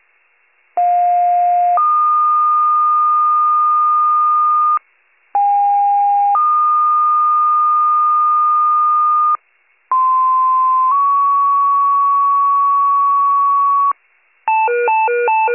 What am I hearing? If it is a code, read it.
Two-tone page, heard as a one-second tone, then a three-second tone, about 710 then 1220 Hz, about 810 then 1220 Hz, about 1040 then 1120 Hz. It is heard from 0.8 to 13.9 s.